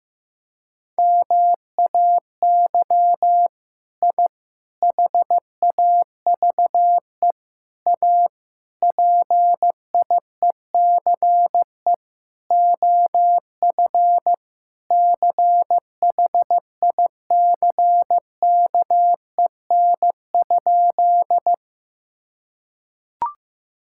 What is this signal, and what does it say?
Morse code, heard as a keyed tone, about 15 wpm, 700 Hz. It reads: MAY I HAVE A PIECE OF CHICKEN?
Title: Sentences from Top 1000 Words - Review Part 5 - 15wpm